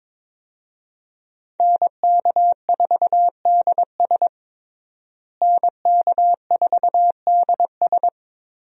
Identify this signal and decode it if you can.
Morse code, heard as a keyed tone, about 22 wpm, 700 Hz. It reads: NK4DS NK4DS
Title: USA Call Signs - 5 Characters - Repeated 2x - 22wpm